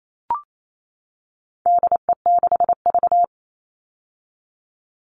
Morse code, heard as a keyed tone, about 28 wpm, 700 Hz. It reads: DE6V
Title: All Cut Numbers - 4-Digit 28wpm